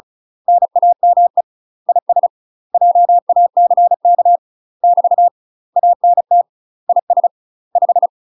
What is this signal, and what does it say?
Morse code, heard as a keyed tone, about 34 wpm, 700 Hz. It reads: NAME IS JACK <BT> ANT IS 5
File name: UK MASTER QSOs TRAINING / 35WPM / UK MASTER QSOs 0039